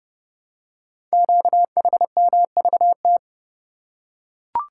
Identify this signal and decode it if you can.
Morse code, heard as a keyed tone, about 30 wpm, 700 Hz. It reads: QHMVT
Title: Code Groups - 5 Letters - 30wpm